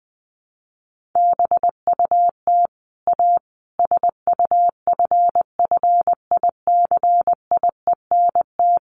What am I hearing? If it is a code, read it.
Morse code, heard as a keyed tone, about 20 wpm, 700 Hz. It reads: BUT A SUFFICIENT
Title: Sentences from Top 1400 Words 20wpm